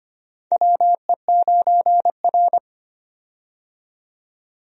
Morse code, heard as a keyed tone, about 25 wpm, 700 Hz. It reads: WE9R
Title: USA Call Signs - 4 Characters - 25wpm